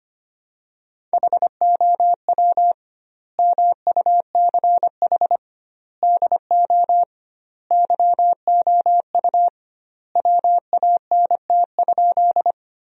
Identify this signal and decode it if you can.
Morse code, heard as a keyed tone, about 25 wpm, 700 Hz. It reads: HOW MUCH DO YOU WANT?
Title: 2x Word-Spacing - Sentences from Top 200 Words - 25wpm